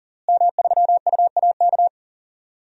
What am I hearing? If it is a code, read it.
Morse code, heard as a keyed tone, about 40 wpm, 700 Hz. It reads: M3UAK
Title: DXCC-100 Call Signs - 5 Characters - Repeated 2x - 40wpm